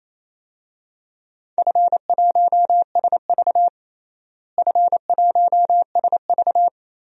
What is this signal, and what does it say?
Morse code, heard as a keyed tone, about 28 wpm, 700 Hz. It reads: F1SV F1SV
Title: DXCC-100 Call Signs - 4 Characters - Repeated 3x - 28wpm